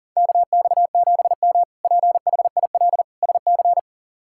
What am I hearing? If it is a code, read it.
Morse code, heard as a keyed tone, about 37 wpm, 690 Hz. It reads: KX7M PHIL SC
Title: K1USN SST QSO Practice - 40wpm